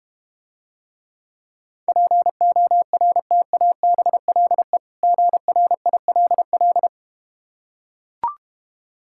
Morse code, heard as a keyed tone, about 31 wpm, 700 Hz. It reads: PORTABLE GRILL